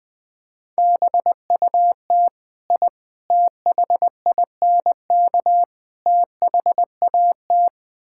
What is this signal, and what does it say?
Morse code, heard as a keyed tone, about 21 wpm, 700 Hz. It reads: BUT I THINK THAT
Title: Build A Phrase - 4-Word Phrases - 20wpm